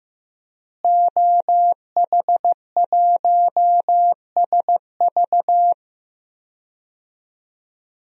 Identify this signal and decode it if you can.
Morse code, heard as a keyed tone, about 15 wpm, 700 Hz. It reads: OH1SV
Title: DXCC-100 Call Signs - 5 Characters - Repeated 2x - 15wpm